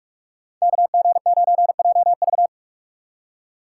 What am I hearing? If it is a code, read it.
Morse code, heard as a keyed tone, about 46 wpm, 700 Hz. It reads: KG9JV
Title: USA Call Signs - 2x Character Spacing - 45wpm